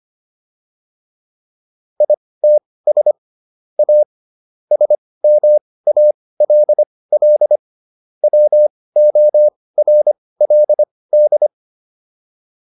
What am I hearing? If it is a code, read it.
Morse code, heard as a keyed tone, about 25 wpm, spaced as for 18 wpm, 600 Hz. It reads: ITS A SMALL WORLD